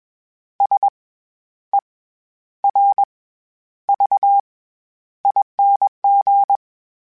Morse code, heard as a keyed tone, about 21 wpm, 800 Hz. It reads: S E R V ING
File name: Special INGs-mp3s / Special INGs-mp3s / SlowToFaster ing @20